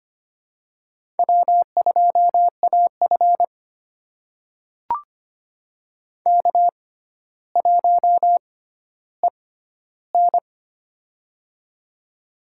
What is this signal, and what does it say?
Morse code, heard as a keyed tone, about 25 wpm, spaced as for 13 wpm, 700 Hz. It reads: W2AF K1EN
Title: USA Call Signs - 4 Characters - 6x Character Spacing - 25wpm